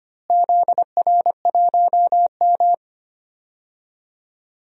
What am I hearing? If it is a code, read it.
Morse code, heard as a keyed tone, about 25 wpm, 700 Hz. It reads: ZR1M
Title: DXCC-100 Call Signs - 4 Characters - Repeated 3x - 25wpm